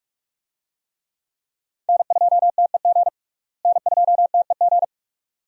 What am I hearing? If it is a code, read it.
Morse code, heard as a keyed tone, about 41 wpm, 700 Hz. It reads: N2TEG N2TEG
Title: USA Call Signs - 5 Characters - Repeated 2x - 45wpm